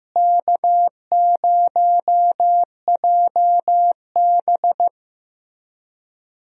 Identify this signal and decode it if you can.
Morse code, heard as a keyed tone, about 15 wpm, 700 Hz. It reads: K0JB
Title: USA Call Signs - 4 Characters - Repeated 3x - 15wpm